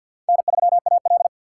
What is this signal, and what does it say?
Morse code, heard as a keyed tone, about 51 wpm, 700 Hz. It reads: N3AL
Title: USA Call Signs - 3x Character Spacing - 50wpm